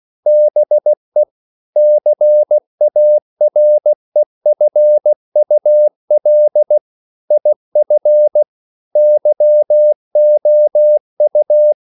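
Morse code, heard as a keyed tone, about 16 wpm, 600 Hz. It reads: BE CAREFUL IF YOU
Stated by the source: English quotations sent in CW as practice at 16 words per minute.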